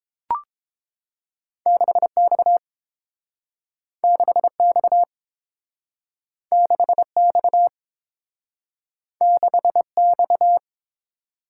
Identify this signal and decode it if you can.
Morse code, heard as a keyed tone, about 30 wpm, 700 Hz. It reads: 6X 6X 6X 6X